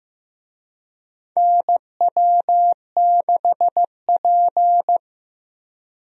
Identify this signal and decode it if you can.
Morse code, heard as a keyed tone, about 15 wpm, 700 Hz. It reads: NW6P
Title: USA Call Signs - 4 Characters - Repeated 3x - 15wpm